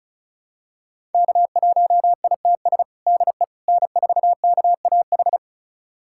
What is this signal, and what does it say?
Morse code, heard as a keyed tone, about 36 wpm, 700 Hz. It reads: K1ITS DE N4KAH
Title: USA Callsign DE Callsign - 5 Characters - 35wpm